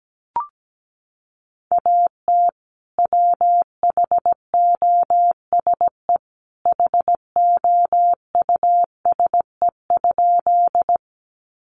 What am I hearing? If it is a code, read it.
Morse code, heard as a keyed tone, about 17 wpm, 700 Hz. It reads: AT WHOSE HOUSE?
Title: Sentences from Top 1200 Words 17wpm